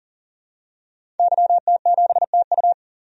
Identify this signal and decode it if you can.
Morse code, heard as a keyed tone, about 40 wpm, 700 Hz. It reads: YT7TU
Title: DXCC-100 Call Signs - 5 Characters - 40wpm